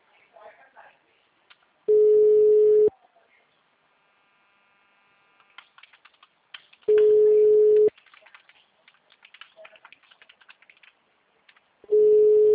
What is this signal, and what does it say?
Ringback tone (Europe, 425 Hz), with speech.
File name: out- -2007- - - .7632